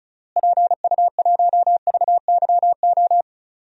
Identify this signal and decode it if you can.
Morse code, heard as a keyed tone, about 34 wpm, 700 Hz. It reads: PU1VYO
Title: DXCC-100 Call Signs - 6 Characters - Repeated 2x - 35wpm